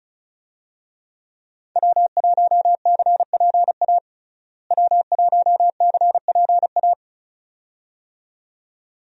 Morse code, heard as a keyed tone, about 34 wpm, 700 Hz. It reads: W1CPA W1CPA
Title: USA Call Signs - 5 Characters - Repeated 2x - 35wpm